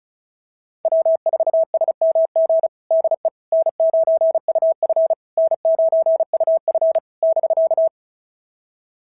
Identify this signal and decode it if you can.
Morse code, heard as a keyed tone, about 36 wpm, 660 Hz. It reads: W4SMG DE N9UF N9UF <BK>
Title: SOTA QSO Practice 35wpm